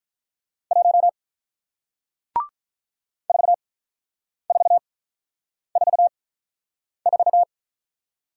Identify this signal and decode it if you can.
Morse code, heard as a keyed tone, about 48 wpm, spaced as for 9 wpm, 700 Hz. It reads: 1 4444